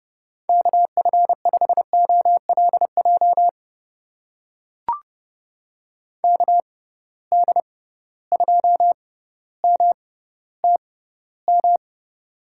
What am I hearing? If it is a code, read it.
Morse code, heard as a keyed tone, about 30 wpm, spaced as for 15 wpm, 700 Hz. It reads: KF5OLJ KD2MTM
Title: USA Call Signs - 6 Characters - 6x Character Spacing - 30wpm